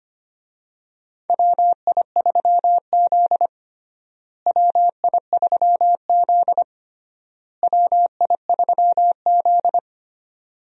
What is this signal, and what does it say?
Morse code, heard as a keyed tone, about 25 wpm, 700 Hz. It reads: WI3Z WI3Z WI3Z
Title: USA Call Signs - 4 Characters - Repeated 3x - 25wpm